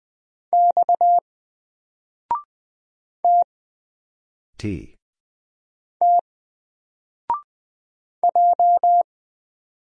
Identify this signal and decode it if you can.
Morse code, heard as a keyed tone, about 20 wpm, 700 Hz. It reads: X T T J
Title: Single Letter-Number 20wpm